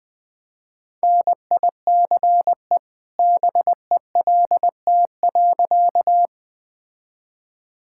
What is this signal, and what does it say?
Morse code, heard as a keyed tone, about 20 wpm, 700 Hz. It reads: NICE BELT.
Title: All-Age Appropriate Jokes - 20wpm